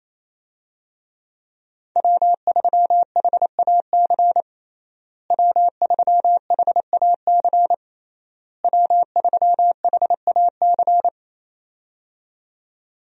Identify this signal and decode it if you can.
Morse code, heard as a keyed tone, about 28 wpm, 700 Hz. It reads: W3HAC W3HAC W3HAC
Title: USA Call Signs - 5 Characters - Repeated 3x - 28wpm